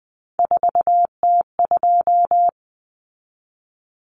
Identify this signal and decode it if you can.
Morse code, heard as a keyed tone, about 20 wpm, 700 Hz. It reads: 4T2
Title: Common Cut Numbers - 3-Digit 20wpm